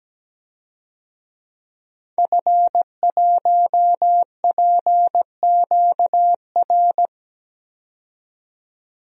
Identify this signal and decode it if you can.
Morse code, heard as a keyed tone, about 17 wpm, 700 Hz. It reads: F1PQR
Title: DXCC-100 Call Signs - 5 Characters - 17wpm